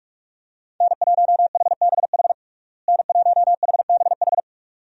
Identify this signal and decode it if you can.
Morse code, heard as a keyed tone, about 44 wpm, 700 Hz. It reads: N1HBH N1HBH
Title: USA Call Signs - 5 Characters - Repeated 2x - 45wpm